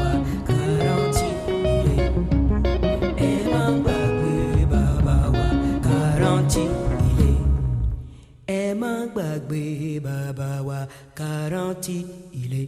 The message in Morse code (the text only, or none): none